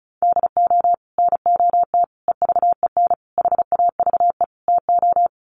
Morse code, heard as a keyed tone, about 34 wpm, 700 Hz. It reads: DO NOT EVEN HAVE TO
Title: Sets of 5 Words Encore2 35wpm